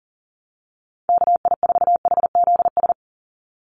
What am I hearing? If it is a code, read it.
Morse code, heard as a keyed tone, about 40 wpm, 700 Hz. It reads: KI4HZS
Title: USA Call Signs 40wpm